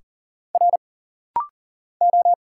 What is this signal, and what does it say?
Morse code, heard as a keyed tone, about 39 wpm, 700 Hz. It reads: R O
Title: Single Letters - Rapid-Fire 40wpm